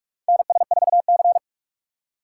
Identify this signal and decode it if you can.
Morse code, heard as a keyed tone, about 43 wpm, 700 Hz. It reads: NS4C
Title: USA Call Signs - 4 Characters - 3x Character Spacing - 45wpm